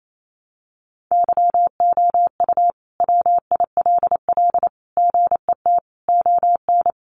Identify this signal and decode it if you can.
Morse code, heard as a keyed tone, about 28 wpm, 700 Hz. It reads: YOU WILL GET ON
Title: Sentences from Top 200 Words 28wpm